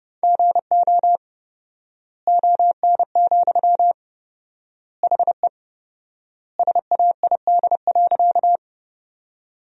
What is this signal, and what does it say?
Morse code, heard as a keyed tone, about 32 wpm, 700 Hz. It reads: GO ON, HE SAID.
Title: 4x Word-Spacing - Sentences from Top 100 Words - 30wpm